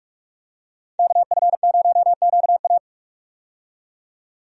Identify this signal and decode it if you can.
Morse code, heard as a keyed tone, about 46 wpm, 700 Hz. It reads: KF0QA
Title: USA Call Signs - 5 Characters - 45wpm